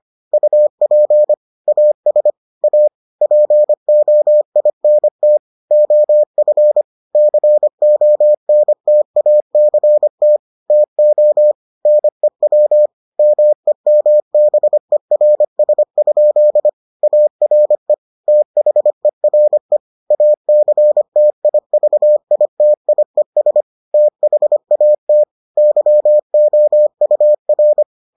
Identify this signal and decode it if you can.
Morse code, heard as a keyed tone, about 25 wpm, 600 Hz. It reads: UP AS A POINT OF CONTACT TO NEW MEMBERS? ARE THERE ACTIVITIES THAT YOUR